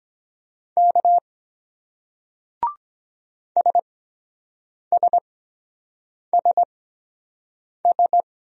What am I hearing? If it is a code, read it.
Morse code, heard as a keyed tone, about 22 wpm, spaced as for 7 wpm, 700 Hz. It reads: K SSSS